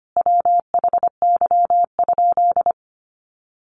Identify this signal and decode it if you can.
Morse code, heard as a keyed tone, about 25 wpm, 700 Hz. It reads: WHY?
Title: Sentences from Top 200 Words 25wpm